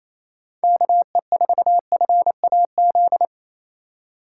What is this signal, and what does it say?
Morse code, heard as a keyed tone, about 28 wpm, 700 Hz. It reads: KE4FAZ
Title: USA Call Signs - 6 Characters - 2x Character Spacing - 28wpm